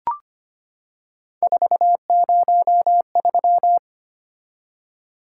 Morse code, heard as a keyed tone, about 25 wpm, 700 Hz. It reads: 403
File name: Three Letter-Number Combinations - 25wpm